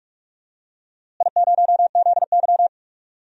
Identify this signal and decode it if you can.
Morse code, heard as a keyed tone, about 44 wpm, 700 Hz. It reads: I0ZY
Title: DXCC-100 Call Signs - 4 Characters - Repeated 3x - 45wpm